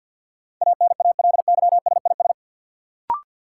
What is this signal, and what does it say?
Morse code, heard as a keyed tone, about 52 wpm, 700 Hz. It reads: ANALYSIS
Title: Sentences from Top 1400 Words - New Word Drill Alt - 50wpm